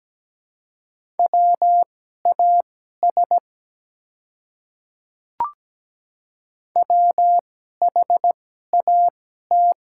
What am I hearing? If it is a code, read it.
Morse code, heard as a keyed tone, about 17 wpm, spaced as for 12 wpm, 700 Hz. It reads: WAS WHAT
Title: 2x Character Spacing - Top 100 Words - 17wpm